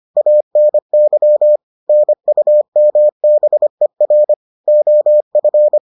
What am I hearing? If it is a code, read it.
Morse code, heard as a keyed tone, about 25 wpm, 600 Hz. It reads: ANY NUMBER OF